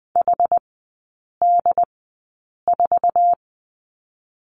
Morse code, heard as a keyed tone, about 20 wpm, spaced as for 8 wpm, 700 Hz. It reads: HD4
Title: 100 Lesson 17e - Letter u - Three Characters 20wpm